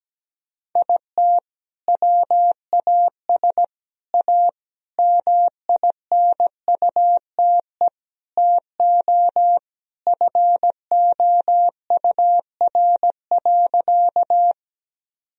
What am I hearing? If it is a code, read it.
Morse code, heard as a keyed tone, about 17 wpm, 700 Hz. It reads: IT WAS A MINUTE TO FOUR.
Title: Sentences from Top 1000 Words - Review Part 1 - 17wpm